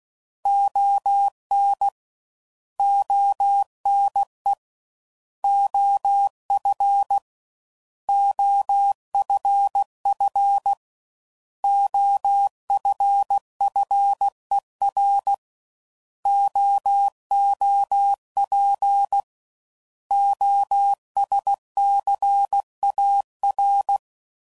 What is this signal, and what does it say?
Morse code, heard as a keyed tone, about 16 wpm, 800 Hz. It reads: ON ONE OF OFF OFFER OOP OSCAR
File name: Prefixes / By First Letter / Os